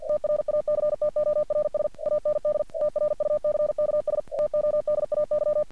Morse code, heard as a keyed tone, about 48 wpm, 600 Hz. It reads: TRACTORS AND TRUCKS TODAY
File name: archived / CRSnet